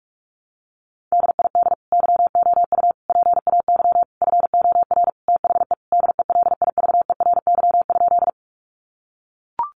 Morse code, heard as a keyed tone, about 44 wpm, 700 Hz. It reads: DID YOU PAY FOR THE DELIVERY?
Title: Sentences from Top 1500 Words 45wpm